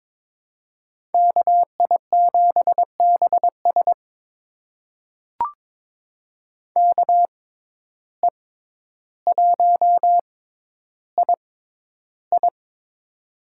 Morse code, heard as a keyed tone, about 22 wpm, spaced as for 11 wpm, 700 Hz. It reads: KI7BS KE1II
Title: USA Call Signs - 5 Characters - 6x Character Spacing - 22wpm